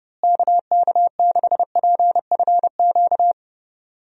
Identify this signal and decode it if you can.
Morse code, heard as a keyed tone, about 30 wpm, 700 Hz. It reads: KK6PFQ
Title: USA Call Signs - 6 Characters - Repeated 3x - 30wpm